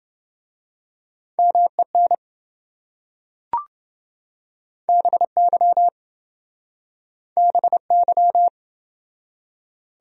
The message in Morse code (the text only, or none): MEN BY BY